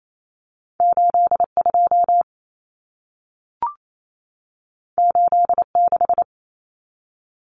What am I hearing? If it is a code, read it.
Morse code, heard as a keyed tone, about 28 wpm, 700 Hz. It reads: 82 86
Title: Numbers - 2-Digit 28wpm